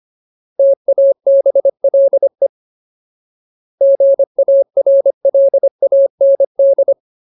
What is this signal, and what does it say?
Morse code, heard as a keyed tone, about 25 wpm, 550 Hz. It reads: TABLE GARLAND
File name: Ultimate Word List / 25WPM Ultimate